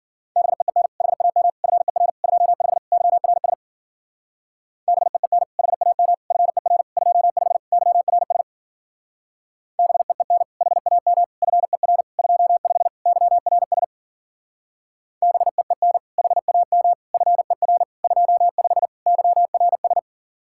Morse code, heard as a keyed tone, about 49 wpm, 700 Hz. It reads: BEEN HAM FER 25 YRS BEEN HAM FER 25 YRS BEEN HAM FER 25 YRS BEEN HAM FER 25 YRS
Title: Speed-Racing Common QSO Phrases Encore - 40wpm